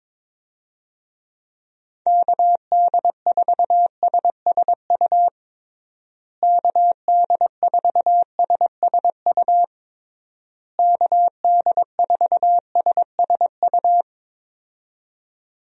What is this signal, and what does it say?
Morse code, heard as a keyed tone, about 22 wpm, 700 Hz. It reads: KD4SSU KD4SSU KD4SSU
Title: USA Call Signs - 6 Characters - Repeated 3x - 22wpm